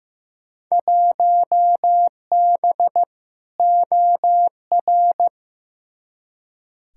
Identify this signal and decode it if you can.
Morse code, heard as a keyed tone, about 15 wpm, 700 Hz. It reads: 1B OR
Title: Field Day Exchanges - 15wpm